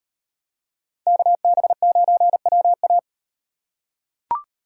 Morse code, heard as a keyed tone, about 38 wpm, 700 Hz. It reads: KB9WA